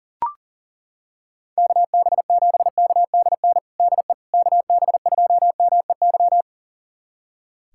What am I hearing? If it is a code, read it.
Morse code, heard as a keyed tone, about 40 wpm, 700 Hz. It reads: KB7KDN DE KB2MEY
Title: USA Callsign DE Callsign - 6 Characters - 40wpm